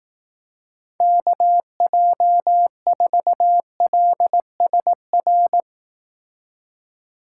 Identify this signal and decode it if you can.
Morse code, heard as a keyed tone, about 18 wpm, 700 Hz. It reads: KJ4LSR
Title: Speed Racing - USA 2x3 Call Signs 15wpm